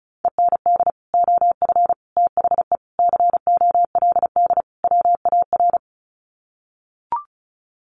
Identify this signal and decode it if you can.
Morse code, heard as a keyed tone, about 34 wpm, 700 Hz. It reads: END OF THE COLD WAR
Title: Sets of 5 Words Encore 35wpm